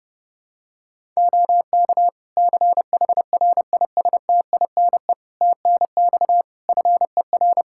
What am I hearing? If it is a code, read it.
Morse code, heard as a keyed tone, about 30 wpm, 700 Hz. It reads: OK CHRISTINE TNX FER
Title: Practice QSO - Exchange 4 - Ending - 30wpm